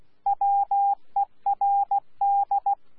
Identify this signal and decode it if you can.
Morse code, heard as a keyed tone, about 16 wpm, 800 Hz. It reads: WERD
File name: Z PI4VRZ / PI4VRZ 14-jun-2025